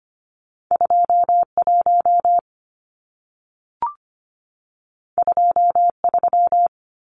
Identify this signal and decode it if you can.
Morse code, heard as a keyed tone, about 25 wpm, 700 Hz. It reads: 21 23
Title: Numbers - 2-Digit 25wpm